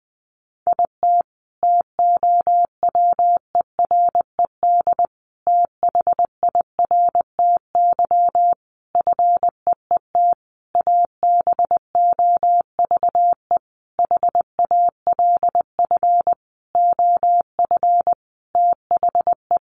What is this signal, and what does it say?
Morse code, heard as a keyed tone, about 20 wpm, 700 Hz. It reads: IT TOWERED THIRTY FEET ABOVE HALF OF THE
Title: A Sound Of Thunder 20wpm